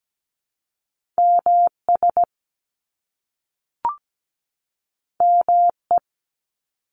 Morse code, heard as a keyed tone, about 17 wpm, 700 Hz. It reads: MS ME
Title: ARRL Sections 17wpm